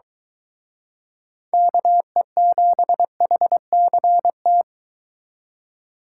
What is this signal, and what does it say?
Morse code, heard as a keyed tone, about 23 wpm, 700 Hz. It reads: KE7HCT